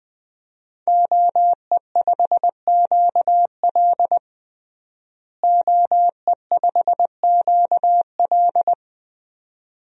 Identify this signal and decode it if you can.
Morse code, heard as a keyed tone, about 20 wpm, 700 Hz. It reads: OE5QL OE5QL
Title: DXCC-100 Call Signs - 5 Characters - Repeated 2x - 20wpm